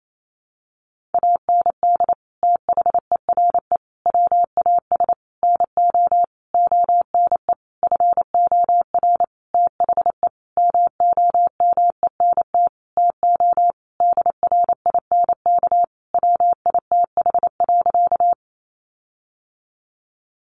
Morse code, heard as a keyed tone, about 28 wpm, 700 Hz. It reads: AND THERE WAS NO ONE FOR THE MOMENT TO DRINK WITH.
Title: Sentences from Top 1500 Words 28wpm